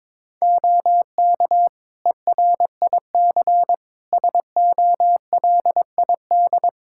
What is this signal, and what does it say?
Morse code, heard as a keyed tone, about 22 wpm, 700 Hz. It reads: OK ERIC SOLID
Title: Practice QSO - Exchange 3 - More Info - 22wpm